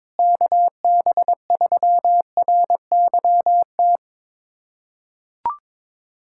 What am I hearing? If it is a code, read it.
Morse code, heard as a keyed tone, about 22 wpm, 700 Hz. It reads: KB3RYT
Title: USA Callsign DE Callsign - 6 Characters - 22wpm